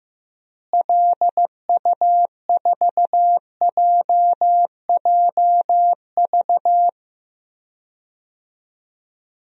Morse code, heard as a keyed tone, about 15 wpm, 700 Hz. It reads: LU4JJV
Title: DXCC-100 Call Signs - 6 Characters - Repeated 3x - 15wpm